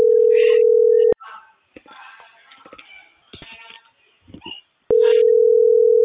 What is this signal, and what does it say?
Ringback tone (North America), 440 + 480 Hz.